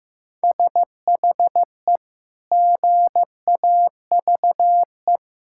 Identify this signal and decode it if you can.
Morse code, heard as a keyed tone, about 15 wpm, 700 Hz. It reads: SHE GAVE
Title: Sentences from Top 1000 Words - Review Part 6 - 15wpm